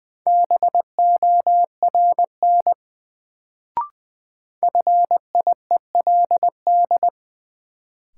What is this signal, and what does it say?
Morse code, heard as a keyed tone, about 20 wpm, 700 Hz. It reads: BORN FIELD
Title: Sentences from Top 1000 Words - New Word Drill - 20wpm